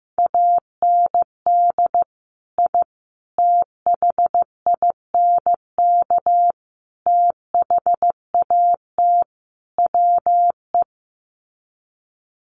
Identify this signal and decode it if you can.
Morse code, heard as a keyed tone, about 15 wpm, 700 Hz. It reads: AND I THINK THAT WE
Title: Sets of 5 Words Encore2 15wpm